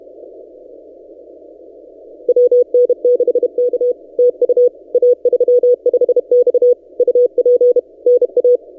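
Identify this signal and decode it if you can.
Morse code, heard as a keyed tone, about 32 wpm, 490 Hz. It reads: WN6K TU A35X UP NA
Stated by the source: Morse from DXpedition station A35X on 12 m, posted by N6ML.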